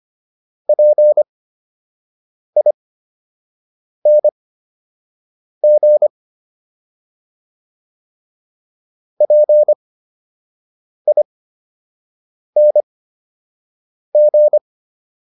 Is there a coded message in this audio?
Yes: Morse code from 0.7 to 14.6 s, heard as a keyed tone, about 25 wpm, spaced as for 6 wpm, 600 Hz. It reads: PING PING